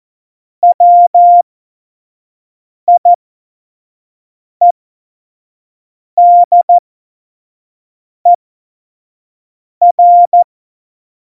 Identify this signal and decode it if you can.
Morse code, heard as a keyed tone, about 14 wpm, spaced as for 5 wpm, 700 Hz. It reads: WIEDER
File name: Morsekurs / Witze / Witz 15. Mai 2018, 25 F70